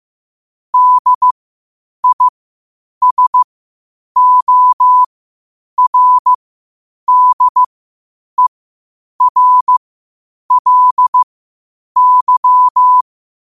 Morse code, heard as a keyed tone, about 15 wpm, spaced as for 8 wpm, 1000 Hz. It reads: DISORDERLY
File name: qotdfiles / 8wpm